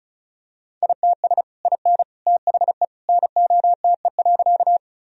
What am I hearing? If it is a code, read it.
Morse code, heard as a keyed tone, about 34 wpm, 700 Hz. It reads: ITS IN THE NOTE.
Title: Sentences from Top 1000 Words - Review Part 2 - 35wpm